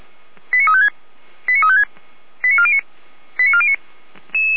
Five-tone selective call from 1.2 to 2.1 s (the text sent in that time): 89356